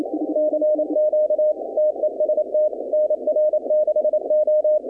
Morse code, heard as a keyed tone, about 27 wpm, 610 Hz. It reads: CQ TEST NR6O
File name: contests / 2019-cw-wpx-kt5j-k5pi- -2330 0-20